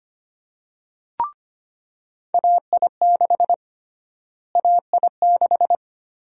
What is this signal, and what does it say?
Morse code, heard as a keyed tone, about 25 wpm, 700 Hz. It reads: AI6 AI6
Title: USA Call Sign Prefixes - Repeated 2x - 25wpm